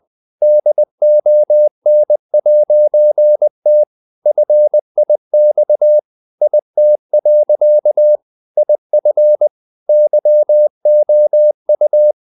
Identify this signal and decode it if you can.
Morse code, heard as a keyed tone, about 20 wpm, 600 Hz. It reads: DON'T FIX IT. IF YOU
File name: CW / Practice / EnglishSayings20